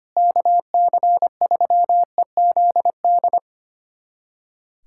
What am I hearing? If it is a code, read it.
Morse code, heard as a keyed tone, about 25 wpm, 700 Hz. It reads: KC3EZD
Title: USA Callsign DE Callsign 25wpm